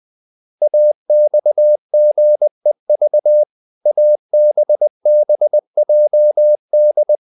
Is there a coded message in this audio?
Yes: Morse code from 0.6 to 7.2 s, heard as a keyed tone, about 20 wpm, 600 Hz. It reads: AXGEV ABBJD